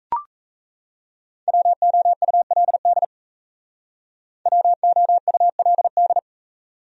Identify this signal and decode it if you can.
Morse code, heard as a keyed tone, about 35 wpm, 700 Hz. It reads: WOULD WOULD